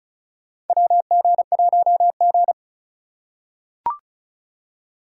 Morse code, heard as a keyed tone, about 34 wpm, 700 Hz. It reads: WG1G
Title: USA Call Signs - 2x Character Spacing - 35wpm